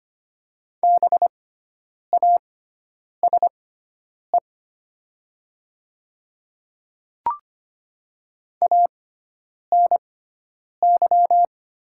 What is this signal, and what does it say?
Morse code, heard as a keyed tone, about 25 wpm, spaced as for 9 wpm, 700 Hz. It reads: BASE ANY
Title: 6x Character Spacing - Top 300 Words - 25wpm